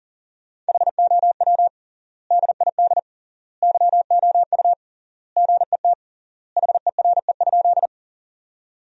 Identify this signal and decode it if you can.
Morse code, heard as a keyed tone, about 40 wpm, 700 Hz. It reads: HOW DID YOU GET HERE?
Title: 3x Word-Spacing - Sentences from Top 200 Words - 40wpm